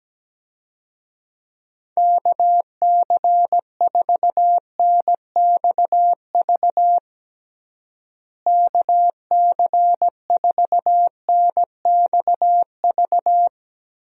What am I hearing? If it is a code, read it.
Morse code, heard as a keyed tone, about 17 wpm, 700 Hz. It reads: KC4NXV KC4NXV
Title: USA Call Signs - 6 Characters - Repeated 3x - 17wpm